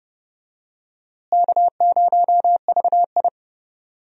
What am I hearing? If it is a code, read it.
Morse code, heard as a keyed tone, about 30 wpm, 700 Hz. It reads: K0VI
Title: USA Call Signs - 4 Characters - Repeated 3x - 30wpm